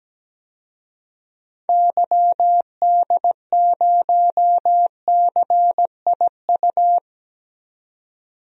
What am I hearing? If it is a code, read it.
Morse code, heard as a keyed tone, about 17 wpm, 700 Hz. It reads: YD0CIU
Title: DXCC-100 Call Signs - 6 Characters - 17wpm